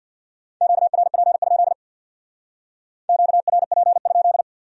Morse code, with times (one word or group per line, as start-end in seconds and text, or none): 0.6-1.7 QRP?
3.1-4.4 QRP?